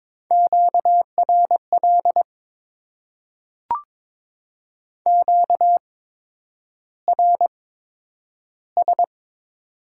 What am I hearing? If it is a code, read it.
Morse code, heard as a keyed tone, about 22 wpm, spaced as for 9 wpm, 700 Hz. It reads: QRL QRS
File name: Top 90 QSO Elements - 8x Character Spacing - 22wpm